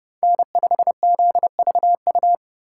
Morse code, heard as a keyed tone, about 26 wpm, 700 Hz. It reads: N5ZVU